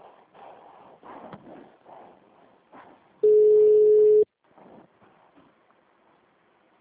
Ringback tone (Europe, 425 Hz).